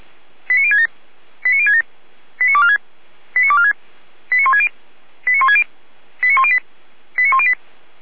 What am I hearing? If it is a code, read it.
Five-tone selective call (ZVEI), heard as quick runs of five tones, tones of about 70 ms. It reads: 89067 89067 89246 89246 89160 89160 89108 89108